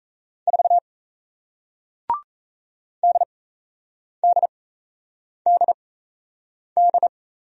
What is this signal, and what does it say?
Morse code, heard as a keyed tone, about 37 wpm, spaced as for 8 wpm, 700 Hz. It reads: 4 DDDD